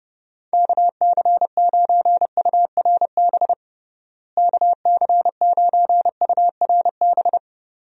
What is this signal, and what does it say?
Morse code, heard as a keyed tone, about 30 wpm, 700 Hz. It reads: KC9URB KC9URB
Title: USA Call Signs - 6 Characters - Repeated 2x - 30wpm